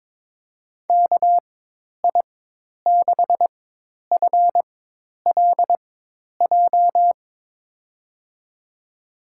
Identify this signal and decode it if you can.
Morse code, heard as a keyed tone, about 23 wpm, spaced as for 10 wpm, 700 Hz. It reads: KI6FLJ
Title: USA Call Signs - 6 Characters - 4x Character Spacing - 22wpm